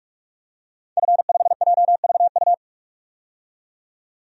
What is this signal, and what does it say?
Morse code, heard as a keyed tone, about 46 wpm, 700 Hz. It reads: F5JVU
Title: DXCC-100 Call Signs - 5 Characters - 45wpm